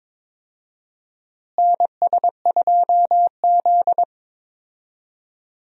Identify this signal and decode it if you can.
Morse code, heard as a keyed tone, about 21 wpm, 700 Hz. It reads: NS2Z
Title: USA Call Signs - 4 Characters - Repeated 2x - 22wpm